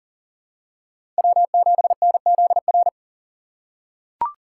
Morse code, heard as a keyed tone, about 39 wpm, 700 Hz. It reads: W7NZR
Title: USA Call Signs - 5 Characters - 3x Character Spacing - 40wpm